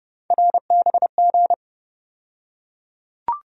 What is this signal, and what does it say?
Morse code, heard as a keyed tone, about 29 wpm, 700 Hz. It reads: RBG
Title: USA Call Sign Postfixes - 3 Letters 30wpm